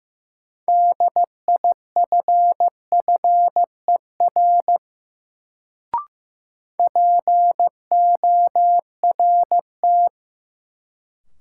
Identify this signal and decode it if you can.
Morse code, heard as a keyed tone, about 15 wpm, 700 Hz. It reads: DIFFER PORT
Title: Sentences from Top 1000 Words - Review Part 1 - New Word Drill - 15wpm